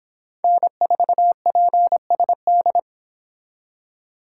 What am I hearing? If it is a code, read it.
Morse code, heard as a keyed tone, about 26 wpm, 700 Hz. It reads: N4PSD